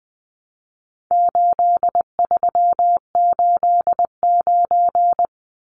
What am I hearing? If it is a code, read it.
Morse code, heard as a keyed tone, about 20 wpm, 700 Hz. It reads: 8389
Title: Numbers - 4-Digit 20wpm